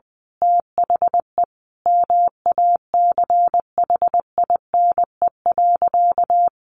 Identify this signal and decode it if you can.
Morse code, heard as a keyed tone, about 20 wpm, 700 Hz. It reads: THE MACHINE.
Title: A Sound Of Thunder 20wpm